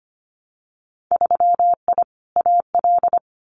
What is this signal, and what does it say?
Morse code, heard as a keyed tone, about 25 wpm, 700 Hz. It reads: 3I AL
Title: Winter Field Day Exchanges Encore 25wpm